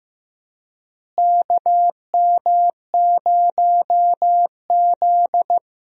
Morse code, heard as a keyed tone, about 15 wpm, 700 Hz. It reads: KM0Z
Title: USA Call Signs - 4 Characters - Repeated 3x - 15wpm